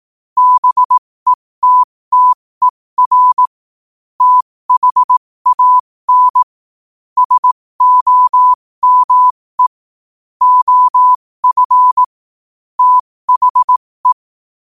Morse code, heard as a keyed tone, about 18 wpm, spaced as for 15 wpm, 1000 Hz. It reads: BETTER THAN SOME OF THE
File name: qotdfiles / 15wpm